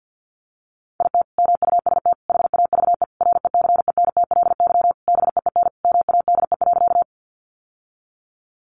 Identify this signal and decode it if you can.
Morse code, heard as a keyed tone, about 50 wpm, 700 Hz. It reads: IT MUST HAVE RECENTLY BEEN MADE.
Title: Sentences from Top 1200 Words 50wpm